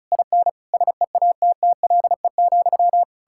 Morse code, heard as a keyed tone, about 44 wpm, 700 Hz. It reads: IN SEATTLE,